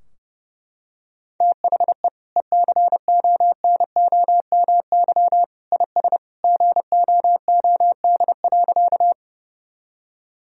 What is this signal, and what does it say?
Morse code, heard as a keyed tone, about 30 wpm, 700 Hz. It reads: THE ECONOMY IS GOOD.
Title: Sentences from Top 1000 Words - Review Part 3 - 30wpm